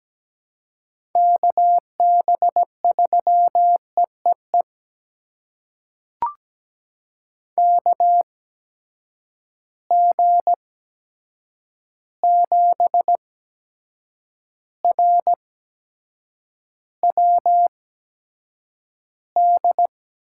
Morse code, heard as a keyed tone, about 17 wpm, spaced as for 7 wpm, 700 Hz. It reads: KB3EEE KG7RWD
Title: USA Call Signs - 6 Characters - 8x Character Spacing - 17wpm